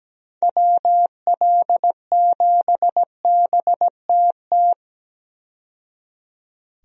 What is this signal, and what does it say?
Morse code, heard as a keyed tone, about 17 wpm, 700 Hz. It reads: WL7BTT
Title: USA Call Signs - 6 Characters - 17wpm